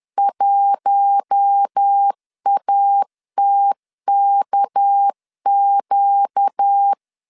Morse code, heard as a keyed tone, about 10 wpm, 800 Hz. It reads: 1ATKQ